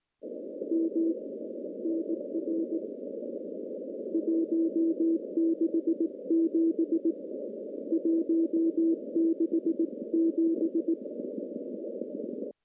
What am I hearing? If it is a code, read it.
Morse code, heard as a keyed tone, about 18 wpm, 330 Hz. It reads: M NR 167 167
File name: contests / CW-SS-k5tr- -0700